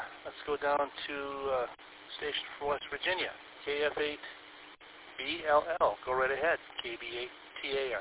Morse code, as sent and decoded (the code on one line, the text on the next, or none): none